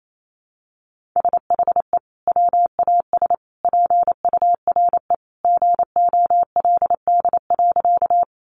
Morse code, heard as a keyed tone, about 28 wpm, 700 Hz. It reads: SHE WAS PURE GOLD.
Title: Sentences from Top 1500 Words 28wpm